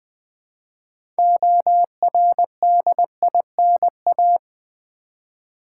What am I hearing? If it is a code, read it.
Morse code, heard as a keyed tone, about 21 wpm, 700 Hz. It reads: ORDINA
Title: Build A Word - 2 Letters At A Time - 8-Letter Words - 20wpm